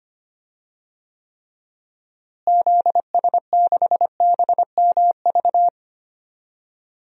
Morse code, heard as a keyed tone, about 25 wpm, 700 Hz. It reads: ZS6BMV
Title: DXCC-100 Call Signs - 6 Characters - Repeated 3x - 25wpm